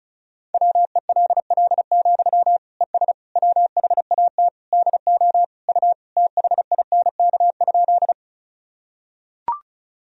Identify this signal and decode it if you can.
Morse code, heard as a keyed tone, about 34 wpm, 700 Hz. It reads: WELL, ES WHAT DO U THINK?
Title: Sentences from Top 200 Words and Abbreviations 35wpm